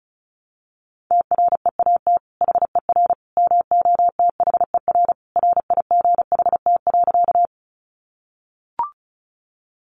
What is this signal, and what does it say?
Morse code, heard as a keyed tone, about 34 wpm, 700 Hz. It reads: TREAT HER MOTHER RIGHT.
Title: Sentences from Top 1300 Words 35wpm